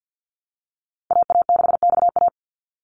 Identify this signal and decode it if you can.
Morse code, heard as a keyed tone, about 50 wpm, 700 Hz. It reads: AA6XA
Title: Call Signs - 50wpm